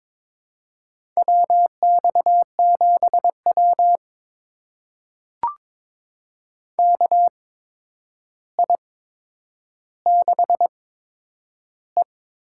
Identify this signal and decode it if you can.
Morse code, heard as a keyed tone, about 22 wpm, spaced as for 9 wpm, 700 Hz. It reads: WX7W KI6E
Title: USA Call Signs - 4 Characters - 8x Character Spacing - 22wpm